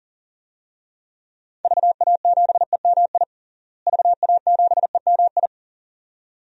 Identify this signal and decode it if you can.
Morse code, heard as a keyed tone, about 40 wpm, 700 Hz. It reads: VA7EMI VA7EMI
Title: DXCC-100 Call Signs - 6 Characters - Repeated 2x - 40wpm